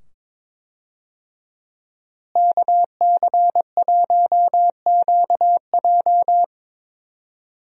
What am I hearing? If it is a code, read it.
Morse code, heard as a keyed tone, about 22 wpm, 700 Hz. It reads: KC1QJ
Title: USA Call Signs - 5 Characters - Repeated 2x - 22wpm